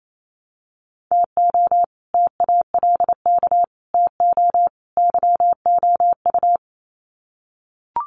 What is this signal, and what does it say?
Morse code, heard as a keyed tone, about 28 wpm, 700 Hz. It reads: TO TALK TO YOU
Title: Sets of 4 Words 28wpm